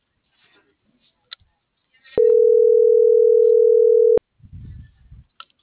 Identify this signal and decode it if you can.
Ringback tone (North America), 440 + 480 Hz.